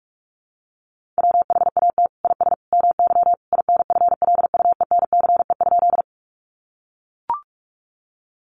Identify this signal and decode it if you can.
Morse code, heard as a keyed tone, about 44 wpm, 700 Hz. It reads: WHAT IS MY INFLUENCE?
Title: Sentences from Top 1100 Words 45wpm